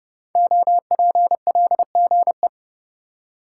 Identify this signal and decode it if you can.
Morse code, heard as a keyed tone, about 30 wpm, 700 Hz. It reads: OPLGE
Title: Code Groups - 5 Letters - 30wpm